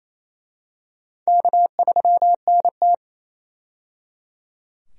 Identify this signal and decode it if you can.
Morse code, heard as a keyed tone, about 28 wpm, 700 Hz. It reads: K3NT
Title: USA Call Signs - 4 Characters - 28wpm